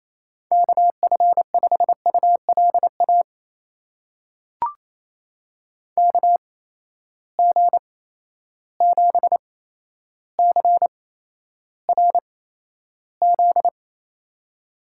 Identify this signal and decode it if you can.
Morse code, heard as a keyed tone, about 28 wpm, spaced as for 12 wpm, 700 Hz. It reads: KF5ULA KG7CRZ
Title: USA Call Signs - 6 Characters - 8x Character Spacing - 28wpm